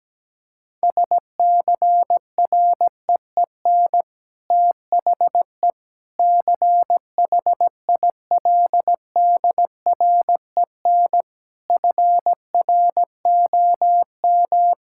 Morse code, heard as a keyed tone, about 17 wpm, 700 Hz. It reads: SCREEN THE CHILDREN FROM
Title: Sentences from Top 1000 Words - Review Part 6 - 17wpm